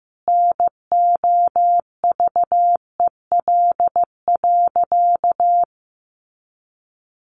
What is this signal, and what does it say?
Morse code, heard as a keyed tone, about 15 wpm, 700 Hz. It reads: NOVEL.
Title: Sentences from Top 1300 Words 15wpm